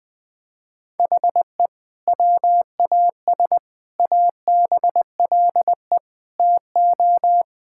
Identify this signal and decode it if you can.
Morse code, heard as a keyed tone, about 21 wpm, 700 Hz. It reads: HE WAS ABLE TO
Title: Sentences with 2-Word Phrases - 20wpm